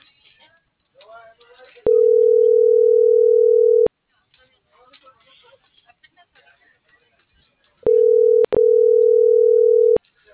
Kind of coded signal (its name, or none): telephone call progress tone